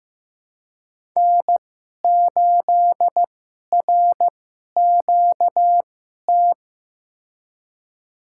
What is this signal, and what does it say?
Morse code, heard as a keyed tone, about 15 wpm, spaced as for 11 wpm, 700 Hz. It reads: N8RQT
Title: USA Call Signs - 5 Characters - 2x Character Spacing - 15wpm